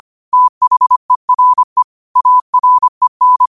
Morse code, heard as a keyed tone, about 25 wpm, 1000 Hz. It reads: THERE AREN